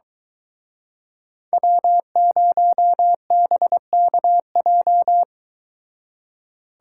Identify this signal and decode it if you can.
Morse code, heard as a keyed tone, about 23 wpm, 700 Hz. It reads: W0BKJ